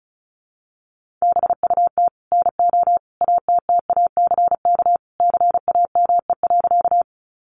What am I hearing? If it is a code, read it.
Morse code, heard as a keyed tone, about 34 wpm, 700 Hz. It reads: BUT NO ATTACK CAME.
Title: Sentences from Top 1500 Words 35wpm